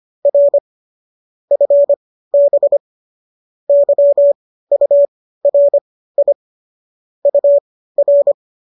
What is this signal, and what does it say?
Morse code, heard as a keyed tone, about 25 wpm, spaced as for 15 wpm, 570 Hz. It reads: R FB YURI UR